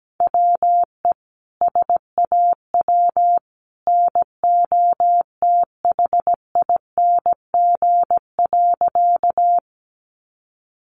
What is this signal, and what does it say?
Morse code, heard as a keyed tone, about 17 wpm, 700 Hz. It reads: WE SAW NOTHING.
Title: A Sound Of Thunder 17wpm